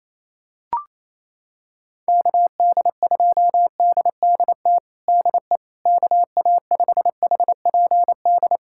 Morse code, heard as a keyed tone, about 28 wpm, 700 Hz. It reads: KD2DDT DE KA5HPD
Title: USA Callsign DE Callsign - 6 Characters - 28wpm